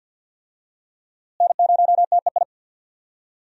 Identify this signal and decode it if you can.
Morse code, heard as a keyed tone, about 52 wpm, 700 Hz. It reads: N0TEI
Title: USA Call Signs - 5 Characters - Repeated 3x - 50wpm